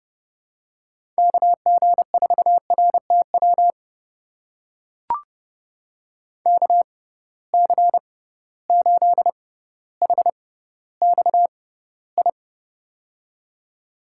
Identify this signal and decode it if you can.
Morse code, heard as a keyed tone, about 31 wpm, spaced as for 16 wpm, 700 Hz. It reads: KG4RTW KC8HXI
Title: USA Call Signs - 6 Characters - 6x Character Spacing - 30wpm